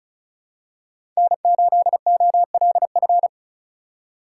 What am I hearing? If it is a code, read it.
Morse code, heard as a keyed tone, about 34 wpm, 700 Hz. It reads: N8OLF
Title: USA Call Signs - 5 Characters - 4x Character Spacing - 35wpm